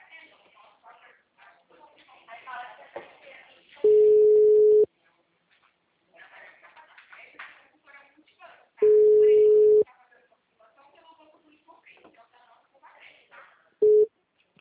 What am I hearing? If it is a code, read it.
Ringback tone (Europe, 425 Hz), with speech.